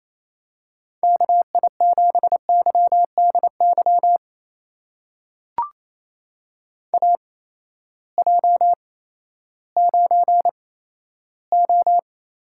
Morse code, heard as a keyed tone, about 28 wpm, spaced as for 14 wpm, 700 Hz. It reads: KI7YDY AJ9O
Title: USA Call Signs - 8x Character Spacing - 28wpm